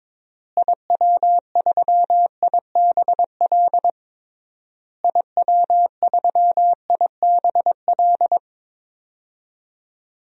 Morse code, heard as a keyed tone, about 22 wpm, 700 Hz. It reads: IW3IBL IW3IBL
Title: DXCC-100 Call Signs - 6 Characters - Repeated 3x - 22wpm